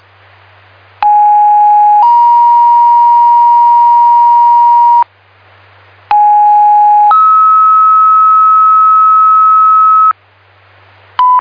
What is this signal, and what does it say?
Two-tone page, heard as a one-second tone, then a three-second tone, about 830 then 950 Hz, about 830 then 1290 Hz. It is heard from 1.0 to 10.1 s.
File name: Sparta EMS